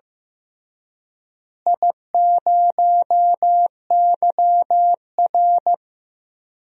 Morse code, heard as a keyed tone, about 15 wpm, 700 Hz. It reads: I0YR
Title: DXCC-100 Call Signs - 4 Characters - Repeated 3x - 15wpm